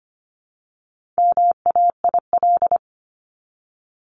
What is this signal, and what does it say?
Morse code, heard as a keyed tone, about 25 wpm, 700 Hz. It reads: MAIL
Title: Top 1751-2000 Words 25wpm